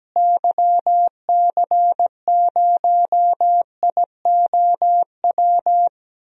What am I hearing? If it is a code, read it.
Morse code, heard as a keyed tone, about 17 wpm, 700 Hz. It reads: YC0IOW
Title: DXCC-100 Call Signs - 6 Characters - Repeated 3x - 17wpm